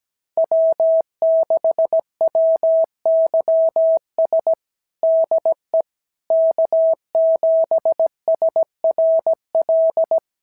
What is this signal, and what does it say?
Morse code, heard as a keyed tone, about 15 wpm, 650 Hz. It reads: W6WYS DE K7SRL
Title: SOTA QSO Practice 17wpm